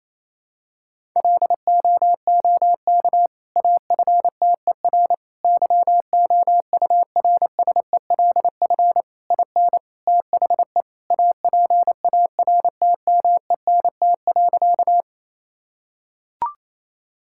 Morse code, heard as a keyed tone, about 28 wpm, 700 Hz. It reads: LOOK AFTER YOURSELF IN THE APARTMENT.
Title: Sentences from Top 1000 Words - Review Part 6 - 28wpm